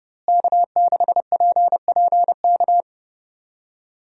Morse code, heard as a keyed tone, about 30 wpm, 700 Hz. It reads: K6PPK
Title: USA Call Signs - 5 Characters - Repeated 3x - 30wpm